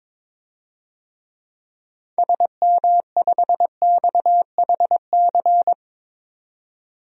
Morse code, heard as a keyed tone, about 22 wpm, 700 Hz. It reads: SM5XHC
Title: DXCC-100 Call Signs - 6 Characters - 22wpm